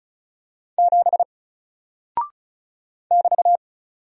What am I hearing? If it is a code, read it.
Morse code, heard as a keyed tone, about 36 wpm, 700 Hz. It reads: 7 <BT>
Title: Single Character (Letter:Number:Punctuation:Prosign) 35wpm